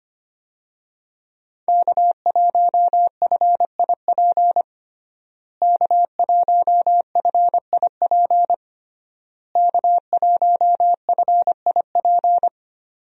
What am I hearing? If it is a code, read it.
Morse code, heard as a keyed tone, about 25 wpm, 700 Hz. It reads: K1FIP K1FIP K1FIP
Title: USA Call Signs - 5 Characters - Repeated 3x - 25wpm